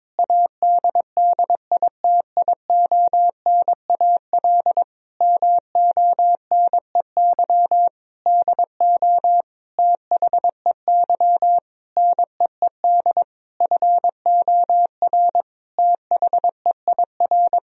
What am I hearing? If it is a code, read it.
Morse code, heard as a keyed tone, about 22 wpm, 700 Hz. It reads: ADDITIONAL MONEY DO THEY NEED FOR THEIR
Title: Sentences from Top 1000 Words - Review Part 4 - 22wpm